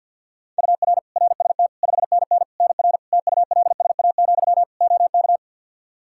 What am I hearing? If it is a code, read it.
Morse code, heard as a keyed tone, about 53 wpm, 700 Hz. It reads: UR RST 5NN NR TULSA, OK
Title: Practice QSO - Exchange 1 - Beginning - 50wpm